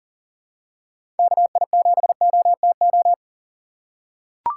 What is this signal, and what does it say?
Morse code, heard as a keyed tone, about 39 wpm, 700 Hz. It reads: KI7OTO
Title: USA Call Signs - 6 Characters - 2x Character Spacing - 40wpm